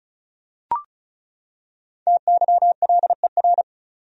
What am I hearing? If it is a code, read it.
Morse code, heard as a keyed tone, about 36 wpm, 700 Hz. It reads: TYLER
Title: USA Common Names - Max 5 Letters - 35wpm